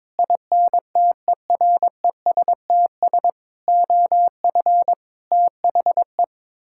Morse code, heard as a keyed tone, about 22 wpm, 700 Hz. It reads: INTERESTS OF THE
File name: practice / 22wpm / Sets of 6 Words - 22wpm